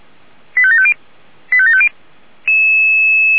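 Five-tone selective call (ZVEI), heard as quick runs of five tones, tones of about 70 ms. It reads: 85750 85750
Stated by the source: ZVEI five-tone selective call 85754 received by SDR.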